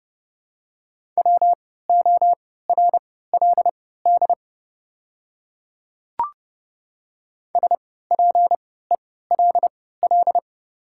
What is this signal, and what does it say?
Morse code, heard as a keyed tone, about 30 wpm, spaced as for 17 wpm, 690 Hz. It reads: WORLD SPELL
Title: 3x Character Spacing - Top 200 Words - 30wpm